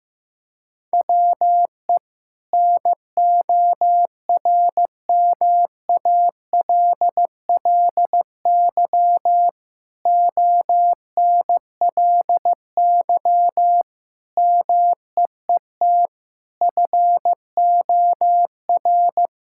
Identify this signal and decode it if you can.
Morse code, heard as a keyed tone, about 15 wpm, 700 Hz. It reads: WE NORMALLY ONLY MEET FOR
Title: Sentences from Top 1000 Words - Review Part 6 - 15wpm